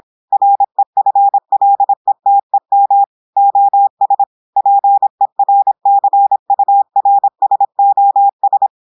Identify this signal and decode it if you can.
Morse code, heard as a keyed tone, about 26 wpm, 800 Hz. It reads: REFLETEM OS PERCURSOS
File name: audios / portugais / portugais texte en clair 1 vitesse 26 mots minutes